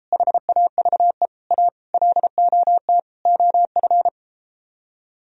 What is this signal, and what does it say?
Morse code, heard as a keyed tone, about 33 wpm, 700 Hz. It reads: HAVE A LOT OF